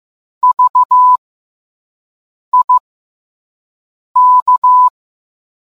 Morse code, heard as a keyed tone, about 15 wpm, spaced as for 5 wpm, 1000 Hz. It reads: VIK